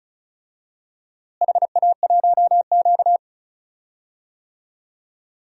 Morse code, heard as a keyed tone, about 34 wpm, 700 Hz. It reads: HA1Q
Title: DXCC-100 Call Signs - 4 Characters - Repeated 3x - 35wpm